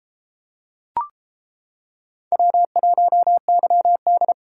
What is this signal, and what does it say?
Morse code, heard as a keyed tone, about 29 wpm, 700 Hz. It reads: W1YD